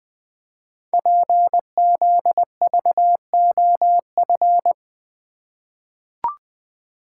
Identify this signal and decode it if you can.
Morse code, heard as a keyed tone, about 20 wpm, 700 Hz. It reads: PZVOF
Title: Code Groups - 5 Letters - 20wpm